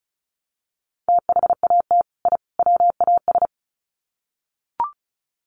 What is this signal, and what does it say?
Morse code, heard as a keyed tone, about 34 wpm, 700 Hz. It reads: THAT I WAS
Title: Sets of 3 Words 35wpm